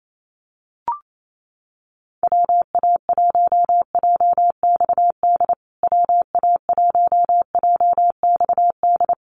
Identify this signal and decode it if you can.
Morse code, heard as a keyed tone, about 28 wpm, 700 Hz. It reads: WA1JXD WA1JXD
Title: USA Call Signs Repeated 2x 28wpm